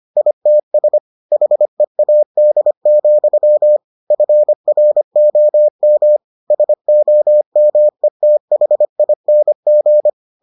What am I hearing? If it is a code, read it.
Morse code, heard as a keyed tone, about 25 wpm, 600 Hz. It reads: ITS HEAD, FROM SOMETHING